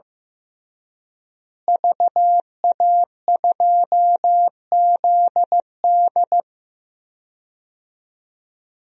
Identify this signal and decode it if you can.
Morse code, heard as a keyed tone, about 15 wpm, 700 Hz. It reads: VA2ZD
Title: DXCC-100 Call Signs - 5 Characters - Repeated 3x - 15wpm